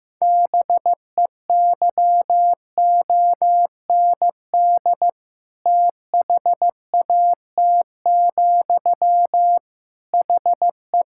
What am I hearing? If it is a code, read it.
Morse code, heard as a keyed tone, about 15 wpm, 700 Hz. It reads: BEYOND THAT, HE
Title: Sentences from Top 1000 Words - Review Part 6 - 15wpm